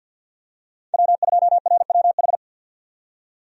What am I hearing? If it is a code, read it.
Morse code, heard as a keyed tone, about 52 wpm, 700 Hz. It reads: W2RWH
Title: USA Call Signs - 5 Characters - 50wpm